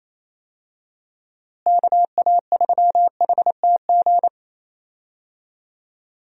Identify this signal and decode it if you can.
Morse code, heard as a keyed tone, about 28 wpm, 700 Hz. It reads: KA3HTG
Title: USA Call Signs - 6 Characters - Repeated 3x - 28wpm